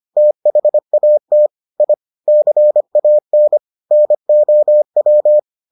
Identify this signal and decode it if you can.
Morse code, heard as a keyed tone, about 25 wpm, 600 Hz. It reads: THAT I CAN NOW